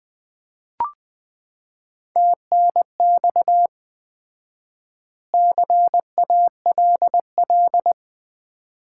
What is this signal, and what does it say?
Morse code, heard as a keyed tone, about 20 wpm, 700 Hz. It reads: TNX CALL
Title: Common QSO Phrases - 4x Word-Spacing - 20wpm